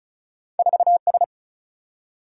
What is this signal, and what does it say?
Morse code, heard as a keyed tone, about 36 wpm, 700 Hz. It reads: 4 S
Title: Single-Letter Single-Number 35wpm